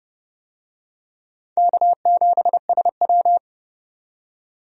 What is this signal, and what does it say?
Morse code, heard as a keyed tone, about 31 wpm, 700 Hz. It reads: K7SW
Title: USA Call Signs - 4 Characters - 30wpm